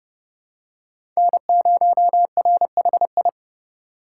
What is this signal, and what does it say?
Morse code, heard as a keyed tone, about 30 wpm, 700 Hz. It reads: N0RHI
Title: USA Call Signs - 5 Characters - 3x Character Spacing - 30wpm